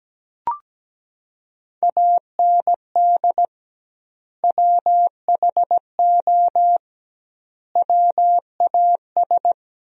Morse code, heard as a keyed tone, about 17 wpm, 700 Hz. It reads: AND WHO WAS
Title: 2x Word-Spacing - Sentences from Top 100 Words - 17wpm